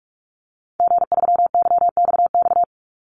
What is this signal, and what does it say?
Morse code, heard as a keyed tone, about 46 wpm, 700 Hz. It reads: G3YXX
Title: Call Signs - 45wpm